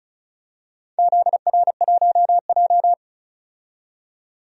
Morse code, heard as a keyed tone, about 36 wpm, 700 Hz. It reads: ZR1J
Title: DXCC-100 Call Signs - 4 Characters - 35wpm